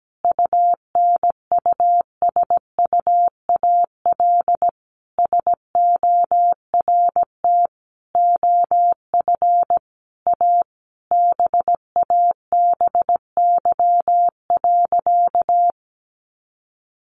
Morse code, heard as a keyed tone, about 17 wpm, 700 Hz. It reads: UNUSUAL SORT OF A BABY.
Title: Sentences from Top 1400 Words 17wpm